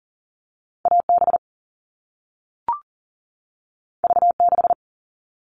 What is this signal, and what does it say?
Morse code, heard as a keyed tone, about 40 wpm, 700 Hz. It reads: AB V6
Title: All Cut Numbers - 2-Digit 40wpm